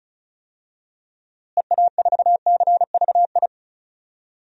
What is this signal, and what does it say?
Morse code, heard as a keyed tone, about 36 wpm, 700 Hz. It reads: EA4CVI
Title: DXCC-100 Call Signs - 6 Characters - Repeated 2x - 35wpm